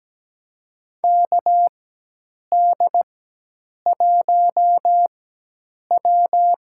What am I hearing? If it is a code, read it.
Morse code, heard as a keyed tone, about 17 wpm, spaced as for 8 wpm, 700 Hz. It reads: KD1W
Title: USA Call Signs - 4x Character Spacing - 17wpm